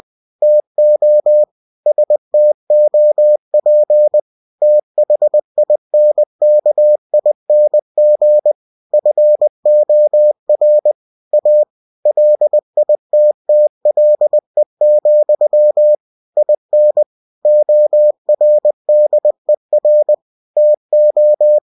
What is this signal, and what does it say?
Morse code, heard as a keyed tone, about 20 wpm, 600 Hz. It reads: TO STOP THINKING FOR A LITTLE, IN ORDER TO
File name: cw-books / winnie-the-pooh / winnie-20wpm-page-0026